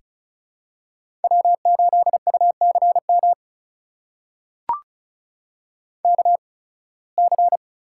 Morse code, heard as a keyed tone, about 36 wpm, spaced as for 23 wpm, 700 Hz. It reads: W8UCM KC
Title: USA Call Signs - 8x Character Spacing - 35wpm